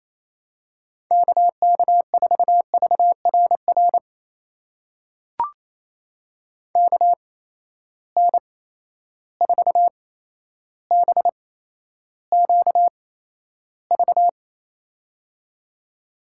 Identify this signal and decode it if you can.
Morse code, heard as a keyed tone, about 28 wpm, spaced as for 12 wpm, 700 Hz. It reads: KK4VRR KN4BQV
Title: USA Call Signs - 6 Characters - 8x Character Spacing - 28wpm